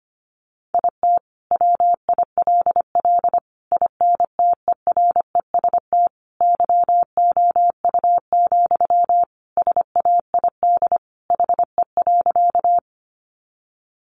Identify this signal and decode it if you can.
Morse code, heard as a keyed tone, about 25 wpm, 700 Hz. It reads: IT WILL INTEREST YOU, SAID HE.
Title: Sentences from Top 400 Words 25wpm